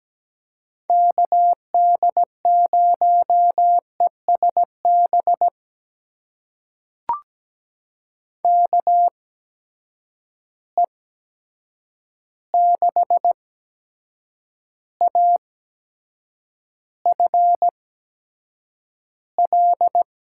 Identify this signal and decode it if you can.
Morse code, heard as a keyed tone, about 17 wpm, spaced as for 7 wpm, 700 Hz. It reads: KD0ESB KE6AFL
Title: USA Call Signs - 6 Characters - 8x Character Spacing - 17wpm